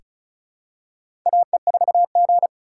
Morse code, heard as a keyed tone, about 34 wpm, 690 Hz. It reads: AE4G
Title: USA Call Signs - 4x Character Spacing - 35wpm